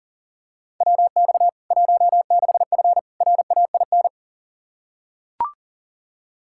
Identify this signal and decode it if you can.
Morse code, heard as a keyed tone, about 37 wpm, 700 Hz. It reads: WX 16F RAIN
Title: K1USN SST QSO Practice 40wpm